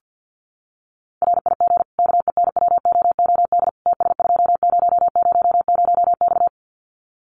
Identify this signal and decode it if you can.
Morse code, heard as a keyed tone, about 50 wpm, 700 Hz. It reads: RIG KENWOOD TS2000X
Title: Common QSO Phrases 50wpm